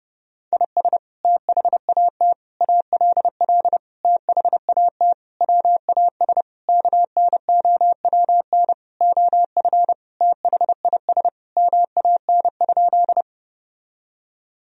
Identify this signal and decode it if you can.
Morse code, heard as a keyed tone, about 30 wpm, 700 Hz. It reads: IS THAT ALL THAT WAS KNOWN OF THIS MAN?
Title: Sentences from Top 1000 Words - Review Part 3 - 30wpm